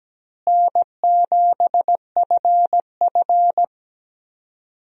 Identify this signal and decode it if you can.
Morse code, heard as a keyed tone, about 17 wpm, 700 Hz. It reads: N7FF
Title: USA Call Signs - 4 Characters - Repeated 2x - 17wpm